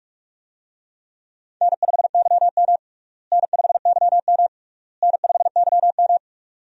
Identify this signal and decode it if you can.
Morse code, heard as a keyed tone, about 41 wpm, 700 Hz. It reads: N5YM N5YM N5YM
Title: USA Call Signs - 4 Characters - Repeated 3x - 45wpm